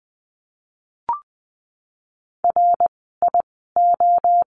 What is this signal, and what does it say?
Morse code, heard as a keyed tone, about 19 wpm, spaced as for 14 wpm, 700 Hz. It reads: RIO
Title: 061 Lesson 11f - Letter h - Words 20wpm